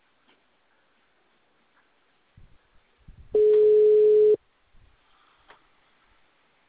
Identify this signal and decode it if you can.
Ringback tone (Europe, 425 Hz).